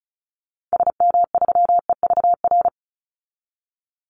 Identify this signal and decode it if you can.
Morse code, heard as a keyed tone, about 34 wpm, 700 Hz. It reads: SM3EVR
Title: Call Signs - 35wpm